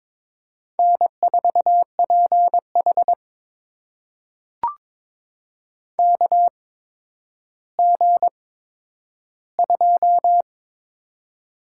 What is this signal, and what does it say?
Morse code, heard as a keyed tone, about 22 wpm, spaced as for 11 wpm, 700 Hz. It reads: N4PH KG2
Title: USA Call Signs - 4 Characters - 8x Character Spacing - 22wpm